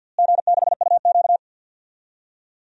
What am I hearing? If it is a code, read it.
Morse code, heard as a keyed tone, about 50 wpm, 700 Hz. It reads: G6UQ
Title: DXCC-100 Call Signs - 4 Characters - Repeated 3x - 50wpm